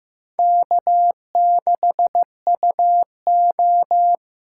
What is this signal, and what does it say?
Morse code, heard as a keyed tone, about 15 wpm, 700 Hz. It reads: K6UO
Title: USA Callsign DE Callsign - 4 Characters - 15wpm